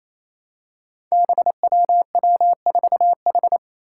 Morse code, heard as a keyed tone, about 28 wpm, 700 Hz. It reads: BWW4H
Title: Five Challenging Letter-Number Combinations 28wpm